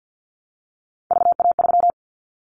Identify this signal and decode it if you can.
Morse code, heard as a keyed tone, about 50 wpm, 700 Hz. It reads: VA3
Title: Callsign Prefixes for Top 50 DXCC - 50wpm